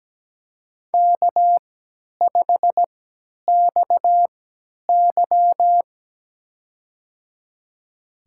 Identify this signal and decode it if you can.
Morse code, heard as a keyed tone, about 17 wpm, spaced as for 10 wpm, 700 Hz. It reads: K5XY
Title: USA Call Signs - 4 Characters - 3x Character Spacing - 17wpm